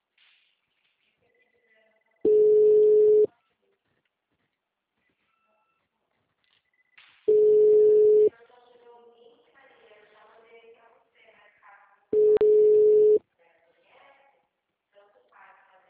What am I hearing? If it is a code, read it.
Ringback tone (Europe, 425 Hz), with speech.